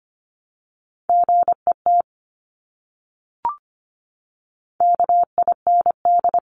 Morse code, heard as a keyed tone, about 25 wpm, 700 Hz. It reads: GET KIND
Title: Top 500 Words Repeated 25wpm